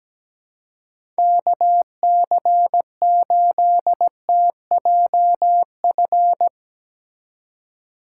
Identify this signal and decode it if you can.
Morse code, heard as a keyed tone, about 17 wpm, 700 Hz. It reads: KC8TJF
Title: USA Call Signs - 2x Character Spacing - 17wpm